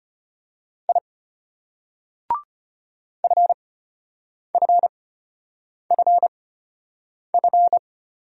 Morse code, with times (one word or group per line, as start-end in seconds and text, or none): 0.9-1.0 I
3.2-7.8 FFFF